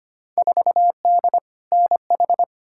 Morse code, heard as a keyed tone, about 25 wpm, 700 Hz. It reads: 4D NH
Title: Field Day Exchanges Encore - 25wpm